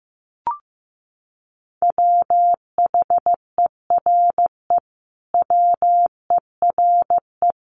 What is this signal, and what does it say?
Morse code, heard as a keyed tone, about 14 wpm, 700 Hz. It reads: WHERE WERE
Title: Sentences from Top 1000 Words 15wpm